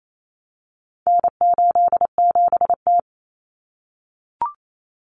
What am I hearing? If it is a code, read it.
Morse code, heard as a keyed tone, about 28 wpm, 700 Hz. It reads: N87T
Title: Common Cut Numbers - 4-Digit 28wpm